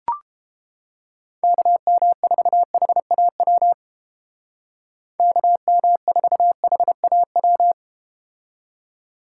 Morse code, heard as a keyed tone, about 30 wpm, 700 Hz. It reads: KM4HAW KM4HAW